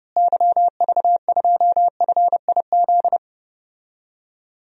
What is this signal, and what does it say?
Morse code, heard as a keyed tone, about 32 wpm, 700 Hz. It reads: YV2FIZ
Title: DXCC-100 Call Signs - 6 Characters - Repeated 2x - 30wpm